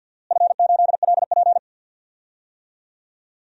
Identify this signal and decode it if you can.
Morse code, heard as a keyed tone, about 46 wpm, 700 Hz. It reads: F8LP
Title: DXCC-100 Call Signs - 4 Characters - Repeated 2x - 50wpm